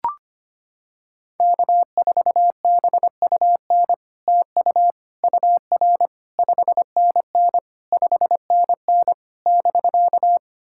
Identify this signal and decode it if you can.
Morse code, heard as a keyed tone, about 25 wpm, 700 Hz. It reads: K4BUN TU UR 5NN 5NN <BK>
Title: SOTA QSO Practice 25wpm